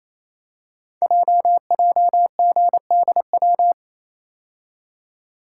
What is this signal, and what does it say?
Morse code, heard as a keyed tone, about 28 wpm, 700 Hz. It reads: JJGDW
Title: Code Groups - 5 Letters - 28wpm